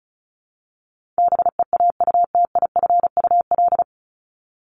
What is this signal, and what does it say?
Morse code, heard as a keyed tone, about 34 wpm, 700 Hz. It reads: BEAUTIFUL
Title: Top 751-1000 Words 35wpm